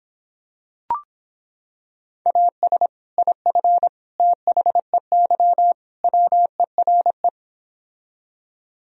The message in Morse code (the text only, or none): AS IF THEY WERE